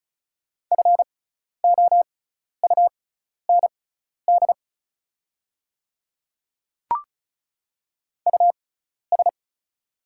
Morse code, heard as a keyed tone, about 36 wpm, spaced as for 12 wpm, 690 Hz. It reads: FOUND US
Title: 6x Character Spacing - Top 300 Words - 35wpm